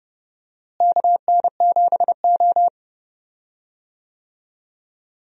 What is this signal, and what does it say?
Morse code, heard as a keyed tone, about 30 wpm, 700 Hz. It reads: KN7O
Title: USA Call Signs - 4 Characters - Repeated 2x - 30wpm